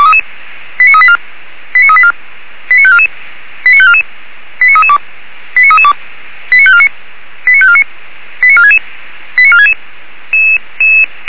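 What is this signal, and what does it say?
Five-tone selective call (ZVEI), heard as quick runs of five tones, tones of about 70 ms. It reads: 89384 89384 89544 89544 89332 89332 89549 89549 89466 89466
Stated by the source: ZVEI five-tone selective call received on SDR.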